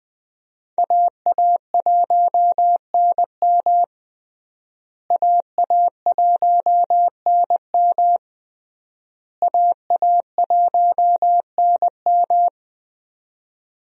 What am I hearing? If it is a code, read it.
Morse code, heard as a keyed tone, about 21 wpm, 700 Hz. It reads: AA1NM AA1NM AA1NM
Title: USA Call Signs - 5 Characters - Repeated 3x - 20wpm